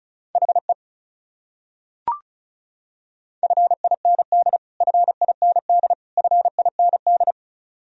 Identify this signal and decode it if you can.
Morse code, heard as a keyed tone, about 36 wpm, 700 Hz. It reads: HE FIND FIND FIND
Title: Top 100 Words Repeated 35wpm